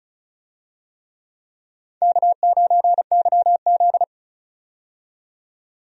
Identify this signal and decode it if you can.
Morse code, heard as a keyed tone, about 36 wpm, 700 Hz. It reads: K9YZ
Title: USA Call Signs - 4 Characters - 35wpm